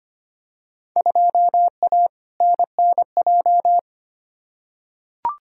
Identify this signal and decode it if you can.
Morse code, heard as a keyed tone, about 25 wpm, 700 Hz. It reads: 2A NNJ
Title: Field Day Exchanges - 25wpm